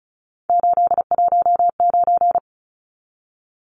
Morse code, heard as a keyed tone, about 34 wpm, 700 Hz. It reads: 819
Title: Numbers - 3-Digit 35wpm